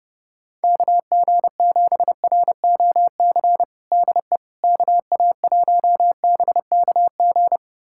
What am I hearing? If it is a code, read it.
Morse code, heard as a keyed tone, about 30 wpm, 700 Hz. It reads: KG7ROC DE KA1BKG
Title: USA Callsign DE Callsign - 6 Characters - 30wpm